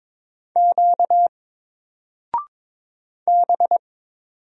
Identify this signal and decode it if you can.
Morse code, heard as a keyed tone, about 22 wpm, 700 Hz. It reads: Q B
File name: Single-Letter-Number-22wpm